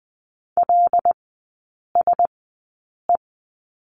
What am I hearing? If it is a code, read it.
Morse code, heard as a keyed tone, about 20 wpm, spaced as for 8 wpm, 700 Hz. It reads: LSE
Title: 178 Lesson 29e - Slash - Three Characters 20wpm